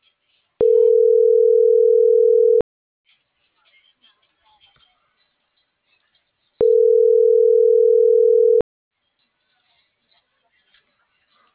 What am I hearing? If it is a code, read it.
Ringback tone (North America), 440 + 480 Hz.
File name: CENTRAL NMP / out patricia.martineznmp - .34959